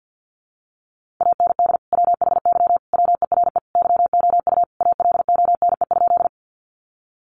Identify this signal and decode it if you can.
Morse code, heard as a keyed tone, about 50 wpm, 700 Hz. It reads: AND WHY WERE YOU ALONE?
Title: Sentences from Top 1000 Words 50wpm